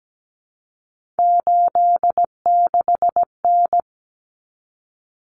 Morse code, heard as a keyed tone, about 17 wpm, 700 Hz. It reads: 86N
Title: Common Cut Numbers - 3-Digit 17wpm